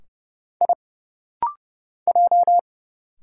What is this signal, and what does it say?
Morse code, heard as a keyed tone, about 29 wpm, 700 Hz. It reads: I J
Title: Single Letters - Rapid-Fire 30wpm